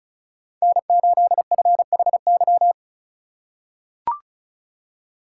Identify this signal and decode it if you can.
Morse code, heard as a keyed tone, about 34 wpm, 700 Hz. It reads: N8FHY
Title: USA Call Signs - 5 Characters - 4x Character Spacing - 35wpm